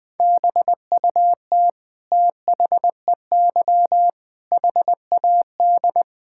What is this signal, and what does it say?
Morse code, heard as a keyed tone, about 20 wpm, 700 Hz. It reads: BUT THEY HAD
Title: Sentences from Top 1000 Words - Review Part 6 - 20wpm